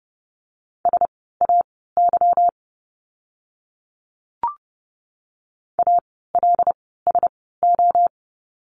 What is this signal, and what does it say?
Morse code, heard as a keyed tone, about 30 wpm, spaced as for 17 wpm, 700 Hz. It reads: SAY ALSO
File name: course / 30:10wpm / 230 Lesson 37f - Letter x - Words 30wpm